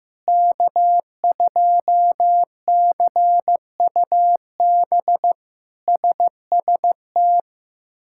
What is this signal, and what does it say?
Morse code, heard as a keyed tone, about 15 wpm, 700 Hz. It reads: K2CUB SST
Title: K1USN SST QSO Practice - 15wpm